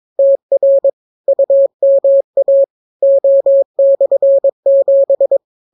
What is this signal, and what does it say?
Morse code, heard as a keyed tone, about 22 wpm, 550 Hz. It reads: TR UMA O/7